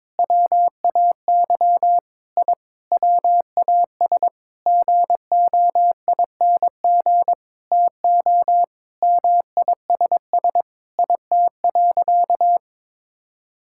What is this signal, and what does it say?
Morse code, heard as a keyed tone, about 22 wpm, 700 Hz. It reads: WAY I WAS GOING TO MISS IT.
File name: Sentences with 7-Word Phrases - 22wpm